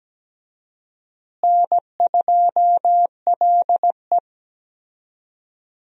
Morse code, heard as a keyed tone, about 17 wpm, 700 Hz. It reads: N2LE
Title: USA Call Signs - 4 Characters - Repeated 3x - 17wpm